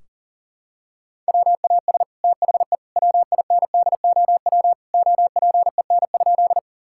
Morse code, heard as a keyed tone, about 40 wpm, 700 Hz. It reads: WAS THE WINDOW OPEN?
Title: Sentences from Top 1600 Words 40wpm